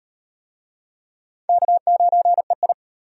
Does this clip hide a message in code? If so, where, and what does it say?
Yes: Morse code from 1.5 to 2.7 s, heard as a keyed tone, about 37 wpm, 700 Hz. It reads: K9EI